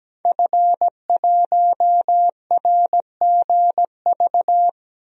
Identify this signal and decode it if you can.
Morse code, heard as a keyed tone, about 17 wpm, 700 Hz. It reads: F1RGV
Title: DXCC-100 Call Signs - 5 Characters - 17wpm